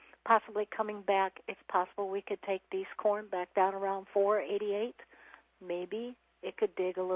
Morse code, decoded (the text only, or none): none